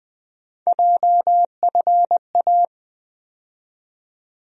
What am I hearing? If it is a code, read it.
Morse code, heard as a keyed tone, about 20 wpm, 700 Hz. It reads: JFA
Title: USA Call Sign Postfixes - 3 Letters 20wpm